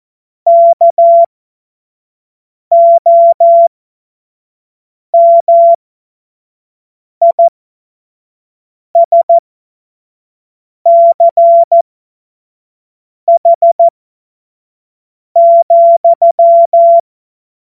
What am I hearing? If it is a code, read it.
Morse code, heard as a keyed tone, about 14 wpm, spaced as for 5 wpm, 680 Hz. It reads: KOMISCH,
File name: Morsekurs / Witze / Witz 19. Oktober 2021, 25 F70